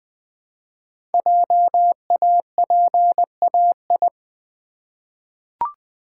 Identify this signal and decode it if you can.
Morse code, heard as a keyed tone, about 20 wpm, 700 Hz. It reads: JAPAI
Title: Code Groups - 5 Letter:Numbers - 20wpm